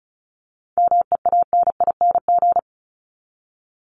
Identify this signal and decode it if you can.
Morse code, heard as a keyed tone, about 34 wpm, 700 Hz. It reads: MEANING
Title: Top 751-1000 Words 35wpm